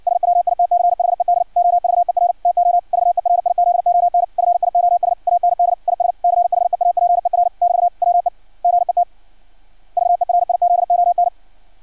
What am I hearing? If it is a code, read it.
Morse code, heard as a keyed tone, about 56 wpm, 700 Hz. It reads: SWITCHED OVER TO VERIZON FIOS AND IS CHEAPER <BT> WE GET VERIZON
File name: archived / CRSnet